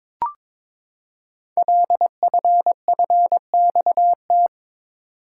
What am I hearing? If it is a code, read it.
Morse code, heard as a keyed tone, about 22 wpm, 700 Hz. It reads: LFFXT
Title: Code Groups - 5 Letters - 22wpm